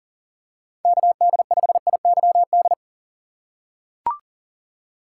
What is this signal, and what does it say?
Morse code, heard as a keyed tone, about 37 wpm, 700 Hz. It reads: KD5IYD
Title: USA Call Signs - 6 Characters - 3x Character Spacing - 40wpm